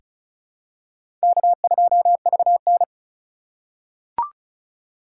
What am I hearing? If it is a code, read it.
Morse code, heard as a keyed tone, about 36 wpm, 690 Hz. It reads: K2VN
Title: USA Call Signs - 4 Characters - 2x Character Spacing - 35wpm